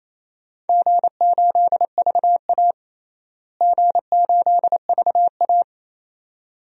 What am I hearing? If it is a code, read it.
Morse code, heard as a keyed tone, about 28 wpm, 700 Hz. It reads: G8VA G8VA
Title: DXCC-100 Call Signs - 4 Characters - Repeated 2x - 28wpm